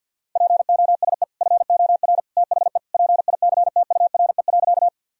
Morse code, heard as a keyed tone, about 49 wpm, 700 Hz. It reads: POSE FOR THE PICTURE.
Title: Sentences from Top 1000 Words - Review Part 2 - 50wpm